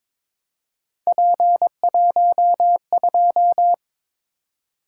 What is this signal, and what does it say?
Morse code, heard as a keyed tone, about 22 wpm, 700 Hz. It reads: P12
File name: practice / 22wpm / Three Letter-Number Combinations - 22wpm